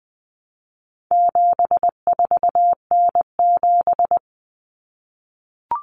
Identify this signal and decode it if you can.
Morse code, heard as a keyed tone, about 20 wpm, 700 Hz. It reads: 74N7
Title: Common Cut Numbers - 4-Digit 20wpm